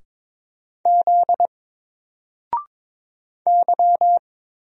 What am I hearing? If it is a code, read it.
Morse code, heard as a keyed tone, about 22 wpm, 700 Hz. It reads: Z Y
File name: Single-Letter-Number-22wpm